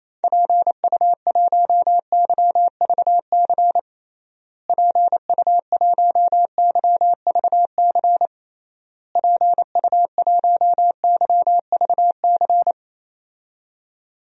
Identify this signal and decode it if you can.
Morse code, heard as a keyed tone, about 27 wpm, 700 Hz. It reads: PU1YVC PU1YVC PU1YVC
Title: DXCC-100 Call Signs - 6 Characters - Repeated 3x - 28wpm